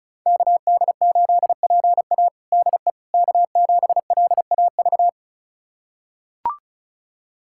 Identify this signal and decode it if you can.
Morse code, heard as a keyed tone, about 36 wpm, 700 Hz. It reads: KD8PA DE K7LAV
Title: USA Callsign DE Callsign - 5 Characters - 35wpm